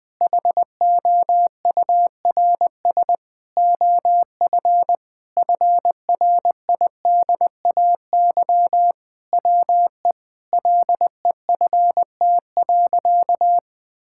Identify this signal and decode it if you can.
Morse code, heard as a keyed tone, about 20 wpm, 700 Hz. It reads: HOURS OF FRIDAY WE LEFT.
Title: Sentences with 6-Word Phrases - 20wpm